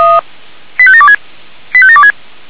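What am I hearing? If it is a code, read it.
Five-tone selective call (ZVEI), heard as quick runs of five tones, tones of about 70 ms. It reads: 85726 85726
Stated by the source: ZVEI five-tone call received by SDR, from feuerwehr-orlamuende.org fire brigade archive.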